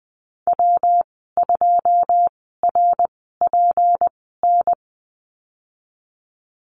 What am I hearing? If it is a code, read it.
Morse code, heard as a keyed tone, about 20 wpm, spaced as for 14 wpm, 700 Hz. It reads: W2RPN
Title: 173 Lesson 28g - Number 9 - Call Signs 20wpm